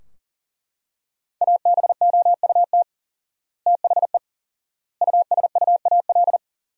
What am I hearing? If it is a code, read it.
Morse code, heard as a keyed tone, about 40 wpm, 700 Hz. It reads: ABOUT THE USUAL
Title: 4x Word-Spacing - Sentences from Top 400 Words - 40wpm